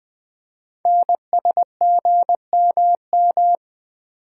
Morse code, heard as a keyed tone, about 20 wpm, 700 Hz. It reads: NSGMM
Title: Code Groups - 5 Letters - 20wpm